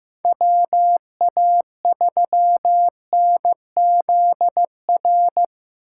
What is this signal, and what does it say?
Morse code, heard as a keyed tone, about 15 wpm, 700 Hz. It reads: WA3NZR
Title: K1USN SST QSO Practice - 15wpm